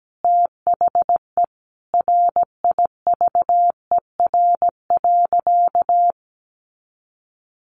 Morse code, heard as a keyed tone, about 17 wpm, 700 Hz. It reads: THE RIVER.
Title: Sentences from Top 1300 Words 17wpm